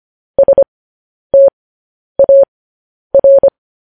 Morse code, heard as a keyed tone, about 25 wpm, spaced as for 10 wpm, 550 Hz. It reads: STAR
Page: news headlines in morse code at 10 wpm